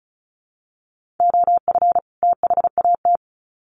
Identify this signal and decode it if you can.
Morse code, heard as a keyed tone, about 34 wpm, 700 Hz. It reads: OF THAT
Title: Sets of 2 Words 35wpm